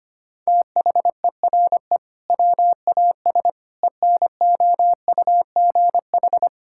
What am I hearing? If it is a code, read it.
Morse code, heard as a keyed tone, about 25 wpm, 700 Hz. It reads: THERE WAS ENOUGH
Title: Sentences from Top 1000 Words - Review Part 6 - 25wpm